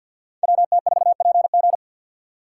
Morse code, heard as a keyed tone, about 64 wpm, 700 Hz. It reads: WT4PG